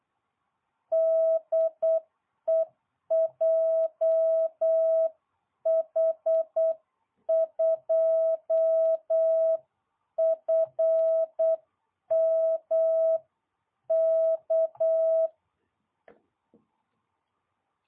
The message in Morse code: DE JH2FMK

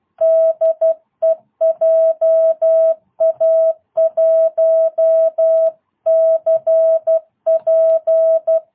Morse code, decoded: DE JA1CP

